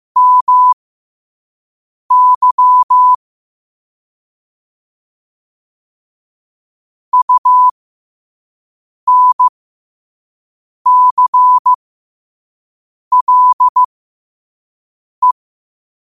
Morse code, heard as a keyed tone, about 15 wpm, spaced as for 5 wpm, 1000 Hz. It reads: MY UNCLE
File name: qotdfiles / 5wpm